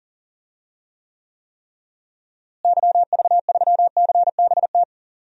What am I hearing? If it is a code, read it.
Morse code, heard as a keyed tone, about 40 wpm, 700 Hz. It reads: YV3CBT
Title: DXCC-100 Call Signs - 6 Characters - Repeated 3x - 40wpm